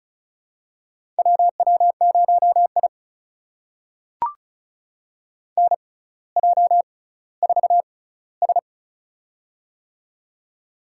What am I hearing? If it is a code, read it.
Morse code, heard as a keyed tone, about 36 wpm, spaced as for 18 wpm, 700 Hz. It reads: WW0I NJ4S
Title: USA Call Signs - 4 Characters - 6x Character Spacing - 35wpm